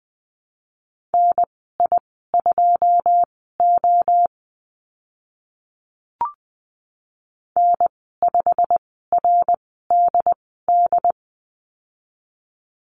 Morse code, heard as a keyed tone, about 20 wpm, spaced as for 15 wpm, 700 Hz. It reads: NI2O N5RDD
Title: 088 Lesson 15g - Number 5 - Call Signs 20wpm